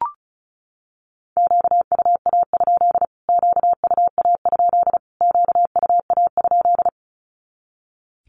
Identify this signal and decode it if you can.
Morse code, heard as a keyed tone, about 34 wpm, 700 Hz. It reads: QUA? QUA? QUA?
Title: Q-Codes Repeated 3x 35wpm